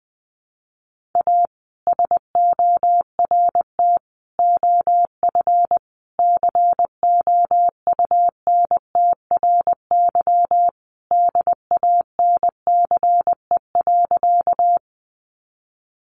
Morse code, heard as a keyed tone, about 20 wpm, 700 Hz. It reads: A SORT OF COUNTRY DANCE.
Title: Sentences from Top 1300 Words 20wpm